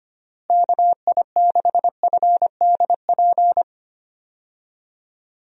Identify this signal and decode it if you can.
Morse code, heard as a keyed tone, about 25 wpm, 700 Hz. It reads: KI6FDP
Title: USA Call Signs - 6 Characters - Repeated 3x - 25wpm